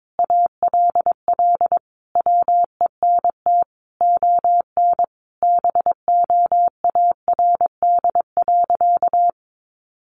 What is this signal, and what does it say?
Morse code, heard as a keyed tone, about 22 wpm, 700 Hz. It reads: ALL WENT ON BOARD.
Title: Sentences from Top 1400 Words 22wpm